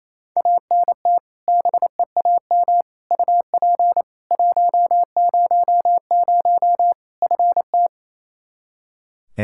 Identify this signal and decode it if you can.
Morse code, heard as a keyed tone, about 28 wpm, 700 Hz. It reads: ANT BEAM UP 100 FT
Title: K1USN SST QSO Practice 28wpm